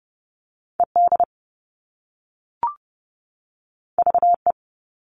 Morse code, heard as a keyed tone, about 30 wpm, 700 Hz. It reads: ED VE
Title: All Cut Numbers - 2-Digit 30wpm